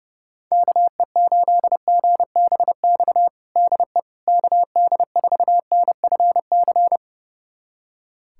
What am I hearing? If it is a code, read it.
Morse code, heard as a keyed tone, about 30 wpm, 700 Hz. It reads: KE8GBX DE KD4NFC
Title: USA Callsign DE Callsign - 6 Characters - 30wpm